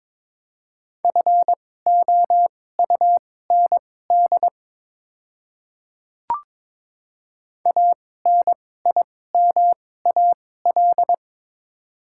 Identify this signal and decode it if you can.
Morse code, heard as a keyed tone, about 22 wpm, spaced as for 16 wpm, 700 Hz. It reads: FOUND ANIMAL
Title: 2x Character Spacing - Top 300 Words - 22wpm